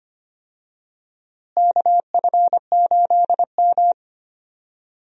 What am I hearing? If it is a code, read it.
Morse code, heard as a keyed tone, about 25 wpm, 700 Hz. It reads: KF8M
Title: USA Call Signs - 4 Characters - Repeated 2x - 25wpm